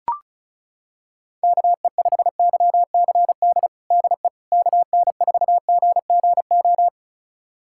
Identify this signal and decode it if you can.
Morse code, heard as a keyed tone, about 34 wpm, 700 Hz. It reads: KE5YCD DE KN4GGO
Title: USA Callsign DE Callsign - 6 Characters - 35wpm